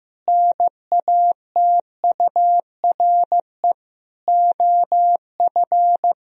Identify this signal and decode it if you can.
Morse code, heard as a keyed tone, about 15 wpm, 700 Hz. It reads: NATURE OF
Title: Sentences from Top 1000 Words - Review Part 4 - 15wpm